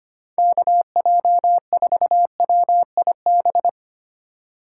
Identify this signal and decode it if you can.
Morse code, heard as a keyed tone, about 31 wpm, 700 Hz. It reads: KJ4WIB
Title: Speed Racing - USA 2x3 Call Signs 25wpm